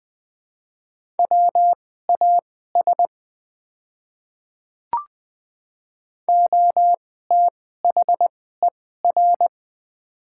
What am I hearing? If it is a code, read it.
Morse code, heard as a keyed tone, about 20 wpm, spaced as for 15 wpm, 700 Hz. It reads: WAS OTHER
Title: 2x Character Spacing - Top 100 Words - 20wpm